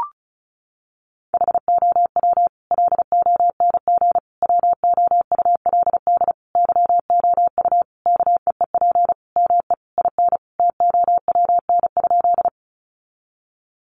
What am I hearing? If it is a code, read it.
Morse code, heard as a keyed tone, about 34 wpm, 700 Hz. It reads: HOW LONG WOULD YOU KEEP ME IN TOWN?
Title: Sentences from Top 500 Words 35wpm